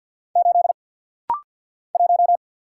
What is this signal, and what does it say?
Morse code, heard as a keyed tone, about 50 wpm, 700 Hz. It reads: 8 1
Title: Single Letter-Number Alt - Rapid-Fire 50wpm